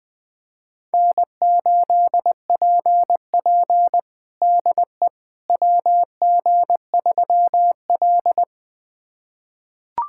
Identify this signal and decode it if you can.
Morse code, heard as a keyed tone, about 20 wpm, 700 Hz. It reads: N8PP DE WG3L
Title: USA Callsign DE Callsign - 4 Characters - 20wpm